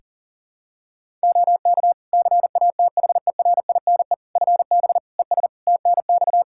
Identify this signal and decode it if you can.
Morse code, heard as a keyed tone, about 37 wpm, 700 Hz. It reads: OK CATHERINE FB ES TNX
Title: Practice QSO - Exchange 2 - Info - 40wpm